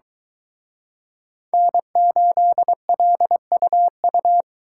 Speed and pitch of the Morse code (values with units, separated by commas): 23 wpm, 700 Hz